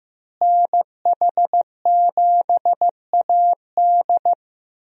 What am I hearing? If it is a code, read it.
Morse code, heard as a keyed tone, about 15 wpm, 700 Hz. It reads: NH7AD
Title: USA Call Signs - 5 Characters - Repeated 3x - 15wpm